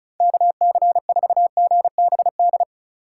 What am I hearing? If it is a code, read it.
Morse code, heard as a keyed tone, about 34 wpm, 700 Hz. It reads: KC4GBD
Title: USA Call Signs - 6 Characters - 2x Character Spacing - 35wpm